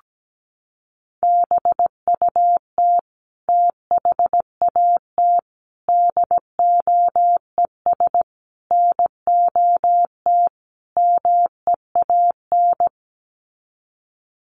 Morse code, heard as a keyed tone, about 17 wpm, 700 Hz. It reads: BUT THAT DOES NOT MEAN
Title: Sets of 5 Words 17wpm